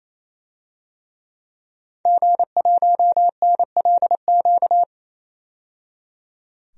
Morse code, heard as a keyed tone, about 28 wpm, 700 Hz. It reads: G1NLQ
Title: DXCC-100 Call Signs - 5 Characters - 28wpm